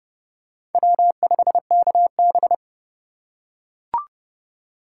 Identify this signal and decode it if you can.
Morse code, heard as a keyed tone, about 30 wpm, 700 Hz. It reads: W5KB
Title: USA Call Signs - 4 Characters - 4x Character Spacing - 30wpm